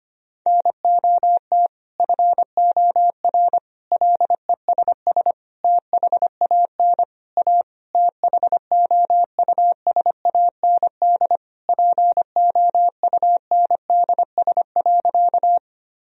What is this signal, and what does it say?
Morse code, heard as a keyed tone, about 25 wpm, 700 Hz. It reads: NOT FOR LESS THAN A THOUSAND POUNDS.
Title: Sentences from Top 1000 Words - Review Part 3 - 25wpm